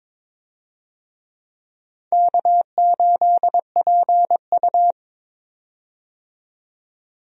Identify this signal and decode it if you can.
Morse code, heard as a keyed tone, about 22 wpm, 700 Hz. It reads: K8PU
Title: USA Call Signs - 4 Characters - 22wpm